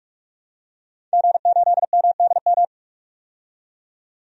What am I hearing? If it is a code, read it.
Morse code, heard as a keyed tone, about 46 wpm, 700 Hz. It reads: G8MDM
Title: DXCC-100 Call Signs - 5 Characters - 45wpm